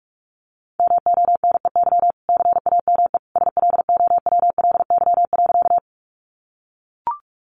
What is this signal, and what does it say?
Morse code, heard as a keyed tone, about 44 wpm, 700 Hz. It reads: MONEY CAME SLOWLY.
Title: Sentences from Top 1100 Words 45wpm